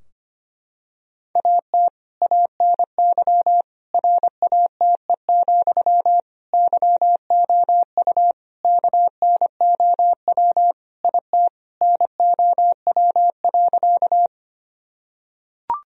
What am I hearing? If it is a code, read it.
Morse code, heard as a keyed tone, about 25 wpm, 700 Hz. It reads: AT ANY RATE, YOU KNOW IT NOW.
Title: Sentences from Top 1000 Words - Review Part 4 - 25wpm